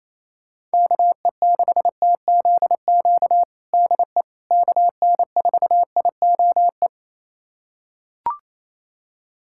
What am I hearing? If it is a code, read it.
Morse code, heard as a keyed tone, about 27 wpm, 700 Hz. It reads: KE6TZQ DE KN4IOE
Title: USA Callsign DE Callsign 28wpm